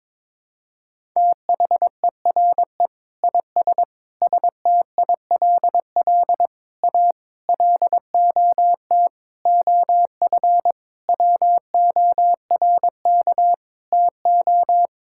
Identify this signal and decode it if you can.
Morse code, heard as a keyed tone, about 22 wpm, 700 Hz. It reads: THERE IS STILL A LOT OF WORK TO
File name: Sentences with 9-Word Phrases - 22wpm